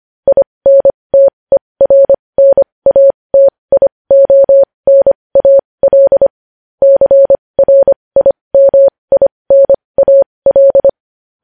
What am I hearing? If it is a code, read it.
Morse code, heard as a keyed tone, about 25 wpm, spaced as for 20 wpm, 550 Hz. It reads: INTERNATIONAL CRIMINAL
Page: news headlines in morse code at 20 wpm